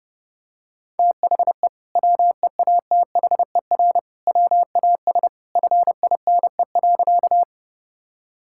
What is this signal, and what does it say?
Morse code, heard as a keyed tone, about 32 wpm, 700 Hz. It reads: THE WEATHER WAS FINE.
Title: Sentences from Top 1000 Words - Review Part 5 - 30wpm